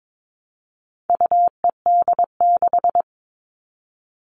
Morse code, heard as a keyed tone, about 22 wpm, 700 Hz. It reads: UED6
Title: All Cut Numbers - 4-Digit 22wpm